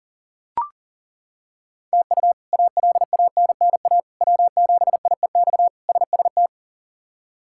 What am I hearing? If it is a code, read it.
Morse code, heard as a keyed tone, about 40 wpm, 700 Hz. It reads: TU ALANNA W7IEX SST
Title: K1USN SST QSO Practice - 40wpm